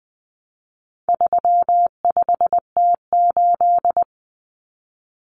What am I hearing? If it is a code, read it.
Morse code, heard as a keyed tone, about 20 wpm, 700 Hz. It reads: 35T8
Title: Common Cut Numbers - 4-Digit 20wpm